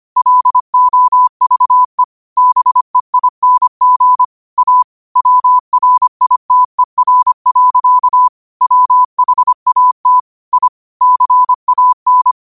Morse code, heard as a keyed tone, about 25 wpm, 1000 Hz. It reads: LOVE BEING A WRITER. WHAT I CAN